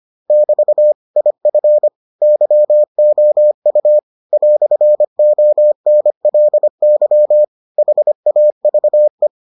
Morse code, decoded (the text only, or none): <BT> IF YOU "ONLY HAVE